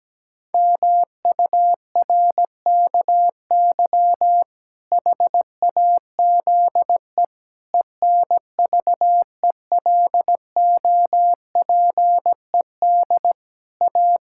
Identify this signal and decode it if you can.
Morse code, heard as a keyed tone, about 17 wpm, 700 Hz. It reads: MURKY HAZE ENVELOPED A
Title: Pangram Challenge v2 - 17wpm